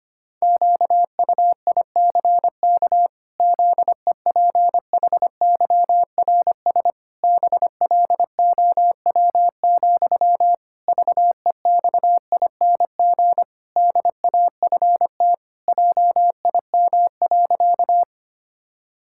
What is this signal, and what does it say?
Morse code, heard as a keyed tone, about 25 wpm, 700 Hz. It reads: QUICK ZEPHYRS BLOW, VEXING DAFT JIM.
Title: Pangram Challenge v2 - 25wpm